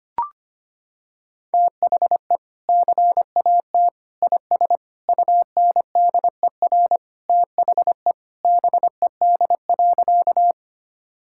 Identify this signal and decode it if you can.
Morse code, heard as a keyed tone, about 25 wpm, 700 Hz. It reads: THE CAT IS UNDER THE BED.
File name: practice / 25wpm / Sentences with 2-Word Phrases - 25wpm